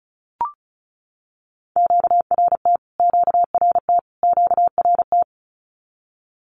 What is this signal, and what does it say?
Morse code, heard as a keyed tone, about 34 wpm, 700 Hz. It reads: QRT QRT QRT
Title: Q-Codes Repeated 3x 35wpm